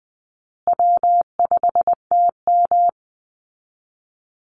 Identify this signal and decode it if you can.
Morse code, heard as a keyed tone, about 20 wpm, 700 Hz. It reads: W5TM
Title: Easy Call Signs Repeated 3x - 20wpm